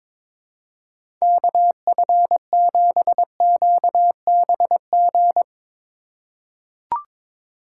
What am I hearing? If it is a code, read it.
Morse code, heard as a keyed tone, about 22 wpm, 700 Hz. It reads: KF7QBG
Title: USA Call Signs - 6 Characters - 4x Character Spacing - 22wpm